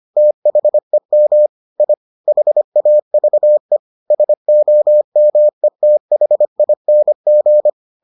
Morse code, heard as a keyed tone, about 25 wpm, 600 Hz. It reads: THEM I HAVE SOMETHING